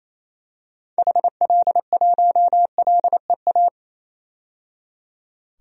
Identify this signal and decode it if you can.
Morse code, heard as a keyed tone, about 28 wpm, 700 Hz. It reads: HL1LEA
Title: DXCC-100 Call Signs - 6 Characters - 28wpm